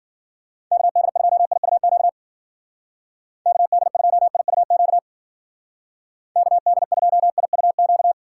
Morse code, heard as a keyed tone, about 53 wpm, 700 Hz. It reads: KD2IUQ KD2IUQ KD2IUQ